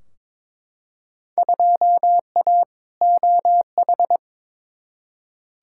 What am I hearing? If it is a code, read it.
Morse code, heard as a keyed tone, about 22 wpm, 700 Hz. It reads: 2A OH
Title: Field Day Exchanges - 22wpm